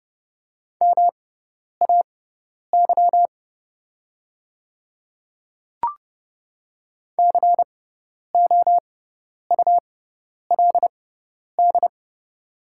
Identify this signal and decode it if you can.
Morse code, heard as a keyed tone, about 31 wpm, spaced as for 10 wpm, 700 Hz. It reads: MAY COULD
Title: 6x Character Spacing - Top 200 Words - 30wpm